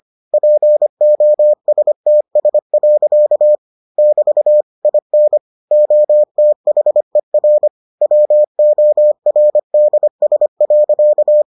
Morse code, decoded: POSTS. <BT> IN OTHER WORDS.

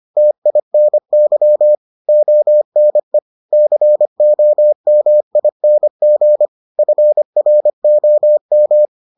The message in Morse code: TINY ONE COMING FROM

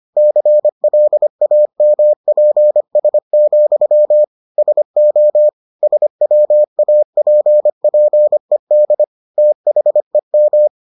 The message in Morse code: CLAMPS, SO SWAPPED THEM

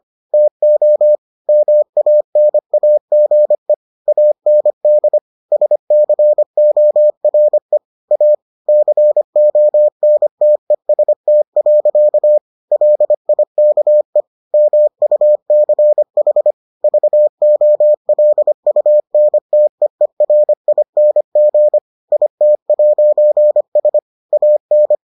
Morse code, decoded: TO MANAGE AND SCORE A CONTEST. LIKE MUCH VOLUNTEERING IT'S AN

